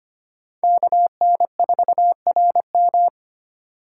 Morse code, heard as a keyed tone, about 25 wpm, 700 Hz. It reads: KN4RM
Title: USA Call Signs - 4x Character Spacing - 25wpm